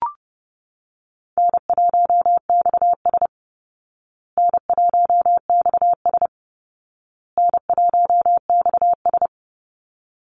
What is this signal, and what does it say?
Morse code, heard as a keyed tone, about 28 wpm, 700 Hz. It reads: N1XS N1XS N1XS
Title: Easy Call Signs Repeated 3x - 30wpm